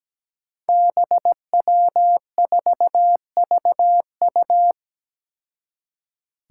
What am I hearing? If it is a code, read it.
Morse code, heard as a keyed tone, about 17 wpm, 700 Hz. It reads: BW4VU
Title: Five Challenging Letter-Number Combinations 17wpm